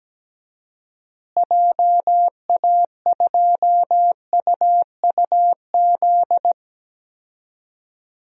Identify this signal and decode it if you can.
Morse code, heard as a keyed tone, about 17 wpm, 700 Hz. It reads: JA2UUZ
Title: DXCC-100 Call Signs - 6 Characters - Repeated 3x - 17wpm